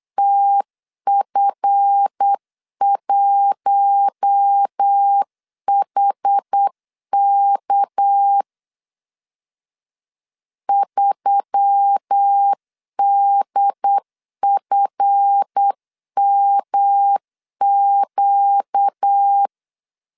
Morse code, heard as a keyed tone, about 8 wpm, 800 Hz. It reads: TF1HK 3DFMQ
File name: LEZIONI / 7 GF9Z / 7 GF 8wpm 3108